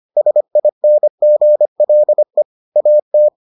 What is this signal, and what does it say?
Morse code, heard as a keyed tone, about 25 wpm, 600 Hz. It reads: SINGLE AT